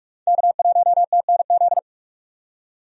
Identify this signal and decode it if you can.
Morse code, heard as a keyed tone, about 46 wpm, 700 Hz. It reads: K1TNZ
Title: USA Call Signs - 5 Characters - 45wpm